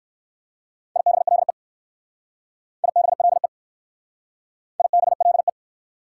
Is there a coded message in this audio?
Yes: Morse code from 1.0 to 5.5 s, heard as a keyed tone, about 60 wpm, 690 Hz. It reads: IBLE IBLE IBLE